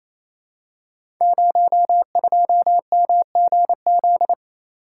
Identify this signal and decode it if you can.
Morse code, heard as a keyed tone, about 28 wpm, 700 Hz. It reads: 02MGZ
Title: Code Groups - 5 Letter:Numbers - 28wpm